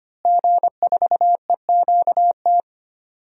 Morse code, heard as a keyed tone, about 25 wpm, 700 Hz. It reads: G4EQT
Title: DXCC-100 Call Signs - 5 Characters - Repeated 2x - 25wpm